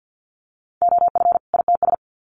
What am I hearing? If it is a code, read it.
Morse code, heard as a keyed tone, about 50 wpm, 700 Hz. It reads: OF ITS
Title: Sets of 2 Words 50wpm